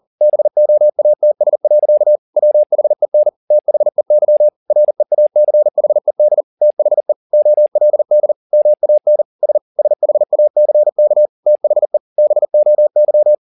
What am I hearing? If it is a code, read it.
Morse code, heard as a keyed tone, about 40 wpm, 600 Hz. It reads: BOATS. WHEN THEY REACHED THE OLD MAN S SHACK THE BOY